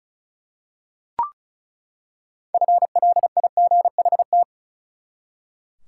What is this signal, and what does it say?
Morse code, heard as a keyed tone, about 36 wpm, 700 Hz. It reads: FLIGHT
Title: Six Letter Words - 35wpm